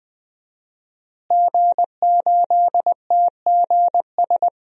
Morse code, heard as a keyed tone, about 20 wpm, 700 Hz. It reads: G8TGS
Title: DXCC-100 Call Signs - 5 Characters - Repeated 2x - 20wpm